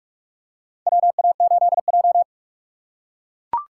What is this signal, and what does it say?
Morse code, heard as a keyed tone, about 44 wpm, 700 Hz. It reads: WA8J
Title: USA Call Signs - 6x Character Spacing - 45wpm